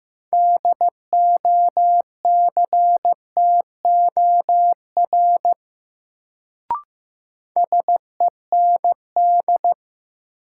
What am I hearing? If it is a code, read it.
Morse code, heard as a keyed tone, about 15 wpm, 700 Hz. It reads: DOCTOR SEND
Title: Sentences from Top 1500 Words - New Word Drill - 15wpm